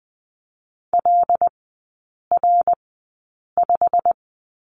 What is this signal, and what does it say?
Morse code, heard as a keyed tone, about 20 wpm, spaced as for 8 wpm, 690 Hz. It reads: LR5
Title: 087 Lesson 15e - Number 5 - Three Characters 20wpm